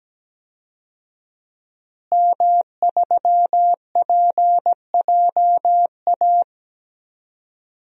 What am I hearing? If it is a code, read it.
Morse code, heard as a keyed tone, about 17 wpm, 700 Hz. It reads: M3PJA
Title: DXCC-100 Call Signs - 5 Characters - Repeated 2x - 17wpm